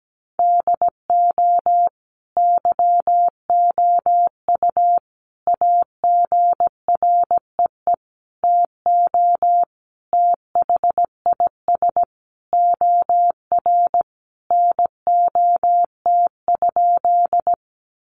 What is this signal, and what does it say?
Morse code, heard as a keyed tone, about 17 wpm, 700 Hz. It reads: DO YOU AGREE TO THIS OR NOT?
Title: Sentences from Top 1100 Words 17wpm